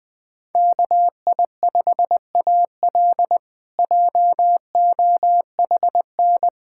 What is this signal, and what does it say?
Morse code, heard as a keyed tone, about 21 wpm, 700 Hz. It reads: KI5AL JOHN
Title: K1USN SST QSO Practice - 20wpm